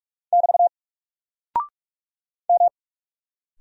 Morse code, heard as a keyed tone, about 44 wpm, 700 Hz. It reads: <BT> M
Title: Single Character (Letter:Number:Punctuation:Prosign) Alt 45wpm